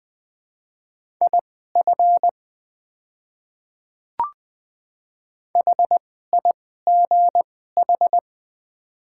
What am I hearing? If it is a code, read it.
Morse code, heard as a keyed tone, about 21 wpm, spaced as for 15 wpm, 700 Hz. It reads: IF HIGH
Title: 2x Character Spacing - Top 200 Words - 20wpm